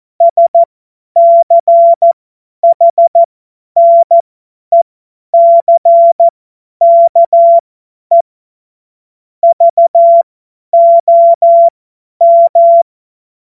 Morse code, heard as a keyed tone, about 14 wpm, spaced as for 10 wpm, 680 Hz. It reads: SCHNECKE VOM
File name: Morsekurs / Witze / Witz 16. Dezember 2025 50 F70